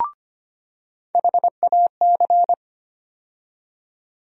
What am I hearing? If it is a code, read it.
Morse code, heard as a keyed tone, about 25 wpm, 700 Hz. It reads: HAC
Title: USA Call Sign Postfixes - 3 Letters 25wpm